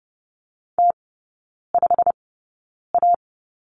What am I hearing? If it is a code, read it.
Morse code, heard as a keyed tone, about 30 wpm, spaced as for 9 wpm, 700 Hz. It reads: T5A
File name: course / 30:10wpm / 087 Lesson 15e - Number 5 - Three Characters 30wpm